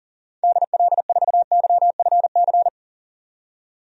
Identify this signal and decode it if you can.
Morse code, heard as a keyed tone, about 37 wpm, 700 Hz. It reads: DL4YFC
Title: DXCC-100 Call Signs - 6 Characters - 40wpm